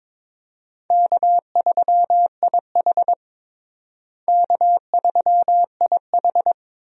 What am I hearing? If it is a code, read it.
Morse code, heard as a keyed tone, about 22 wpm, 700 Hz. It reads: K3IH K3IH
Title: USA Call Signs - 4 Characters - Repeated 3x - 22wpm